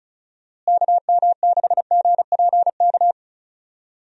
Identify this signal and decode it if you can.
Morse code, heard as a keyed tone, about 34 wpm, 700 Hz. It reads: KM6GPK
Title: USA Call Signs - 6 Characters - Repeated 2x - 35wpm